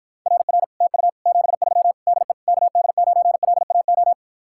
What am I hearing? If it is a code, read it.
Morse code, heard as a keyed tone, about 54 wpm, 700 Hz. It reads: RR TU 73 DE KD9LAO